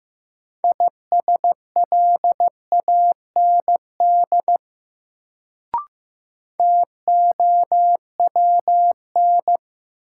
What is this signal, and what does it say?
Morse code, heard as a keyed tone, about 15 wpm, 700 Hz. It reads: ISLAND TOWN
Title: Sentences from Top 500 Words - New Word Drill - 15wpm